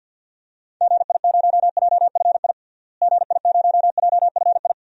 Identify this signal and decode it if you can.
Morse code, heard as a keyed tone, about 50 wpm, 700 Hz. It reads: GI0JFI GI0JFI
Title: DXCC-100 Call Signs - 6 Characters - Repeated 3x - 50wpm